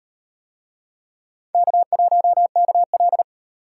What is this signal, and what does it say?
Morse code, heard as a keyed tone, about 38 wpm, 700 Hz. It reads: K1KL